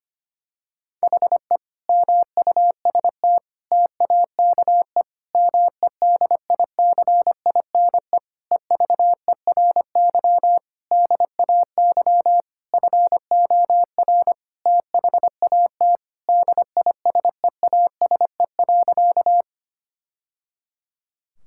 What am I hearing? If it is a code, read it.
Morse code, heard as a keyed tone, about 25 wpm, 700 Hz. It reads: HE MUST TAKE MEDICINE EVERY DAY FOR THAT DISEASE.
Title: Sentences from Top 1000 Words - Review Part 6 - 25wpm